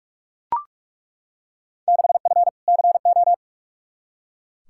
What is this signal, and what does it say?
Morse code, heard as a keyed tone, about 41 wpm, 700 Hz. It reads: 6F CO
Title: Field Day Exchanges Encore - 45wpm